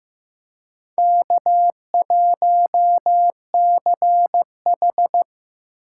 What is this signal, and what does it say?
Morse code, heard as a keyed tone, about 15 wpm, 690 Hz. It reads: K1CH
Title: USA Call Signs - 4 Characters - 2x Character Spacing - 15wpm